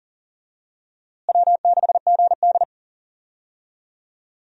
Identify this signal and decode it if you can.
Morse code, heard as a keyed tone, about 40 wpm, 700 Hz. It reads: W6GD
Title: USA Call Signs - 4 Characters - Repeated 2x - 40wpm